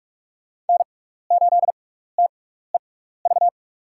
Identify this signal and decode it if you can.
Morse code, heard as a keyed tone, about 44 wpm, spaced as for 15 wpm, 700 Hz. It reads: N8TEV
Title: USA Call Signs - 6x Character Spacing - 45wpm